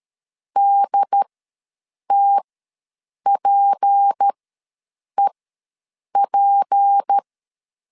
Morse code, heard as a keyed tone, about 13 wpm, spaced as for 7 wpm, 800 Hz. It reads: DTPEP
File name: LEZIONI / 6 PKX0 / 6 P ws 12wpm 3117